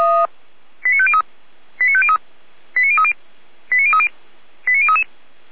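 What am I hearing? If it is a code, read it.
Five-tone selective call (ZVEI), heard as quick runs of five tones, tones of about 70 ms. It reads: 89503 89503 89030 89030 89033
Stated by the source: Five-tone ZVEI selective call received by SDR.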